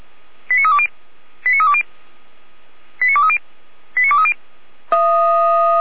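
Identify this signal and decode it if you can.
Five-tone selective call (ZVEI), heard as quick runs of five tones, tones of about 70 ms. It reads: 89320 89320 89230 89230
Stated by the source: Five-tone ZVEI selective call received by SDR, from the Feuerwehr Orlamünde fire brigade audio archive.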